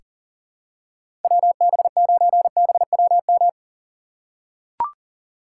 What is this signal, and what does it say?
Morse code, heard as a keyed tone, about 40 wpm, 700 Hz. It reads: WB9BWM
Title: USA Call Signs - 6 Characters - 3x Character Spacing - 40wpm